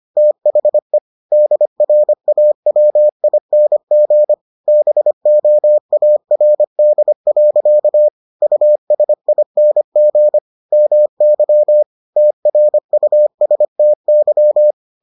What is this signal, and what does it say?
Morse code, heard as a keyed tone, about 25 wpm, 600 Hz. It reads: THE DRAWING BOARD. USING MY TRUSTY